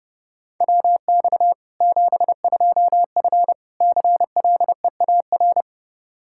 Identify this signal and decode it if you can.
Morse code, heard as a keyed tone, about 30 wpm, 700 Hz. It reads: WX 72F CLEAR
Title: Practice QSO - Exchange 2 - Info - 30wpm